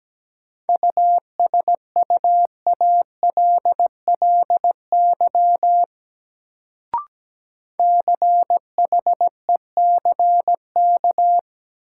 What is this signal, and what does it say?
Morse code, heard as a keyed tone, about 17 wpm, 700 Hz. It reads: USUALLY CHECK
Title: Sentences from Top 1000 Words - Review Part 3 - New Word Drill - 17wpm